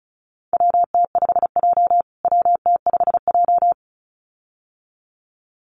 Morse code, heard as a keyed tone, about 34 wpm, 700 Hz. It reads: WT5J WT5J
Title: USA Call Signs Repeated 2x 35wpm